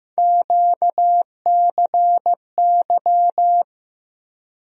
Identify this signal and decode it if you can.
Morse code, heard as a keyed tone, about 15 wpm, 700 Hz. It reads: QCY
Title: USA Call Sign Postfixes - 3 Letters 15wpm